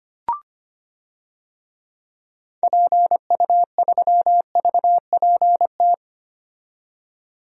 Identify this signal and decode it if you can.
Morse code, heard as a keyed tone, about 25 wpm, 700 Hz. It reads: PU3VPT
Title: DXCC-100 Call Signs - 6 Characters - 25wpm